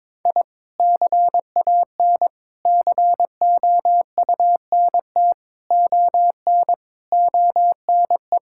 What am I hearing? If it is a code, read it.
Morse code, heard as a keyed tone, about 22 wpm, 700 Hz. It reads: I CAN COUNT ON ONE
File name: Sets of 9 Words - 22wpm